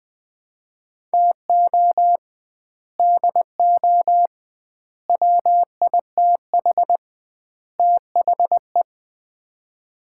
Morse code, heard as a keyed tone, about 20 wpm, 700 Hz. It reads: TO DO WITH THE
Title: 2x Word-Spacing - Sets of 4 Words - 20wpm